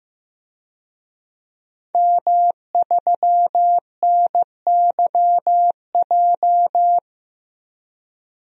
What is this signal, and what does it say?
Morse code, heard as a keyed tone, about 15 wpm, 700 Hz. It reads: M3NYJ
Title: DXCC-100 Call Signs - 5 Characters - Repeated 2x - 15wpm